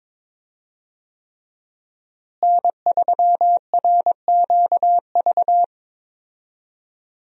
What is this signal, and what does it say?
Morse code, heard as a keyed tone, about 21 wpm, 700 Hz. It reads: N3RQV
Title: USA Call Signs - 5 Characters - Repeated 3x - 22wpm